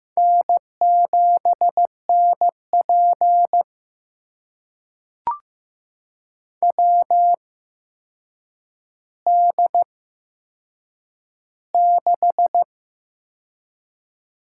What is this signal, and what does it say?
Morse code, heard as a keyed tone, about 15 wpm, spaced as for 7 wpm, 700 Hz. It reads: N7NP WD6
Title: USA Call Signs - 4 Characters - 8x Character Spacing - 15wpm